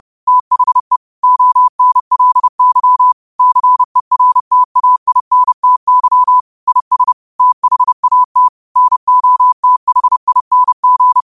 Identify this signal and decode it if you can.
Morse code, heard as a keyed tone, about 30 wpm, 1000 Hz. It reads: THE ONLY CERTAINTY IS THAT NOTHING
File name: qotdfiles / 30wpm